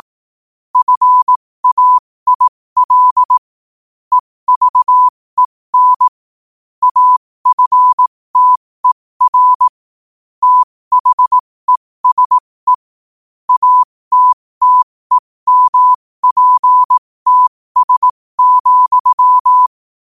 Morse code, heard as a keyed tone, about 18 wpm, spaced as for 15 wpm, 1000 Hz. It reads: FAIL EVEN AFTER THESE ATTEMPTS,
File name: qotdfiles / 15wpm